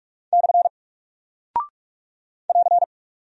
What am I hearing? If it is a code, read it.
Morse code, heard as a keyed tone, about 45 wpm, 700 Hz. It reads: / <AR>
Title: Single Character (Letter:Number:Punctuation:Prosign) Alt 45wpm